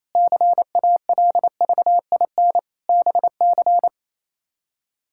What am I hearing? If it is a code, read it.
Morse code, heard as a keyed tone, about 28 wpm, 700 Hz. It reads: CALVIN BC
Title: K1USN SST QSO Practice - 28wpm